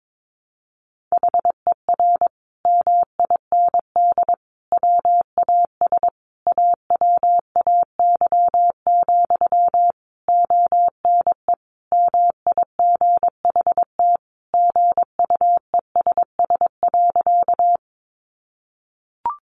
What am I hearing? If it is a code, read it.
Morse code, heard as a keyed tone, about 22 wpm, 700 Hz. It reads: HER MIND WAS AWAY, ONE MIGHT GUESS.
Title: Sentences from Top 1500 Words 22wpm